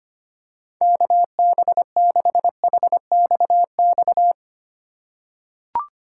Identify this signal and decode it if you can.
Morse code, heard as a keyed tone, about 25 wpm, 700 Hz. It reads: KB6HXX
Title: USA Call Signs - 6 Characters - 4x Character Spacing - 25wpm